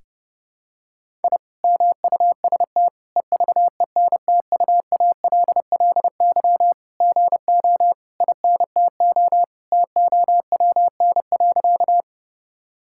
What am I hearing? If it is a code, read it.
Morse code, heard as a keyed tone, about 30 wpm, 700 Hz. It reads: I MUST EVENTUALLY GO INTO TOWN.
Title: Sentences from Top 1000 Words - Review Part 6 - 30wpm